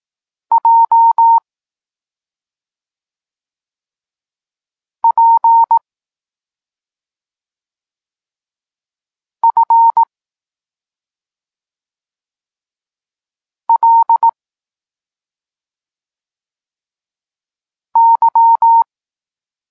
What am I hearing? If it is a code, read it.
Morse code, heard as a keyed tone, about 18 wpm, 920 Hz. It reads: J P F L Y
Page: Tekst 32 – Doskonalenie odbioru poznanych liter